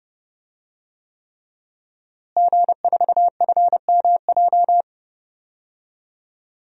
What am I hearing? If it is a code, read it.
Morse code, heard as a keyed tone, about 30 wpm, 700 Hz. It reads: G4FMJ
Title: DXCC-100 Call Signs - 5 Characters - Repeated 3x - 30wpm